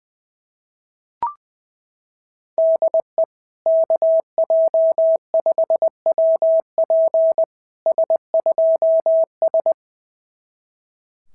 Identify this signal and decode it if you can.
Morse code, heard as a keyed tone, about 19 wpm, 650 Hz. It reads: DE KJ5WP S2S
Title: SOTA QSO Practice 20wpm